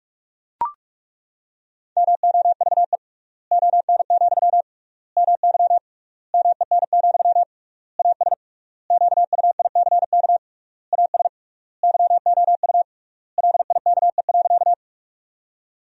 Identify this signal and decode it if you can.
Morse code, heard as a keyed tone, about 42 wpm, 700 Hz. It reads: MOVE ON, MY MEN, AS QUICK AS YOU LIKE.
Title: 3x Word-Spacing - Sentences from Top 500 Words - 45wpm